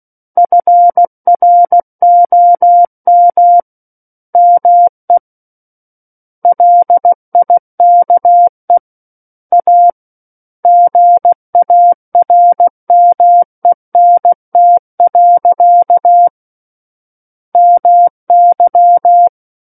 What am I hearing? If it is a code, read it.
Morse code, heard as a keyed tone, about 16 wpm, 700 Hz. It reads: FROM ME LIKE A GARMENT. MY
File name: Book One / 07-Chapter 7 - 16 WPM